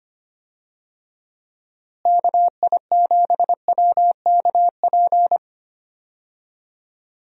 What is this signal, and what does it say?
Morse code, heard as a keyed tone, about 25 wpm, 700 Hz. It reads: KI7WKP
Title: USA Call Signs - 6 Characters - Repeated 2x - 25wpm